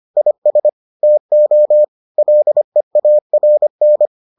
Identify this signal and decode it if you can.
Morse code, heard as a keyed tone, about 25 wpm, 600 Hz. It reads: IS TO LEARN